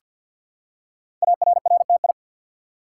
Morse code, heard as a keyed tone, about 48 wpm, 690 Hz. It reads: ARRTI
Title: Code Groups - 5 Letters - 50wpm